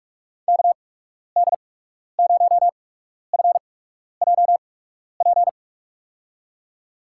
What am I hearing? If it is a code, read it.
Morse code, heard as a keyed tone, about 44 wpm, spaced as for 12 wpm, 700 Hz. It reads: KD0FJP
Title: USA Call Signs - 6 Characters - 8x Character Spacing - 45wpm